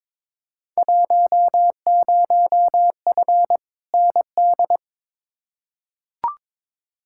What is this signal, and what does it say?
Morse code, heard as a keyed tone, about 22 wpm, 700 Hz. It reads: 10F ND
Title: Field Day Exchanges Encore - 22wpm